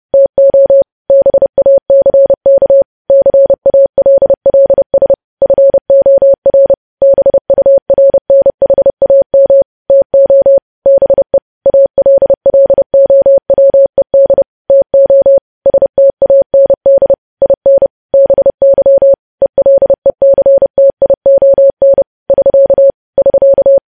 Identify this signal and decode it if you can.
Morse code, heard as a keyed tone, about 30 wpm, 550 Hz. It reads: TO BACK CALLS FOR BURNHAM TO BE ALLOWED TO STAND IN BY ELECTION <SK> <SK>
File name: podcast / News headlines-1x-30x30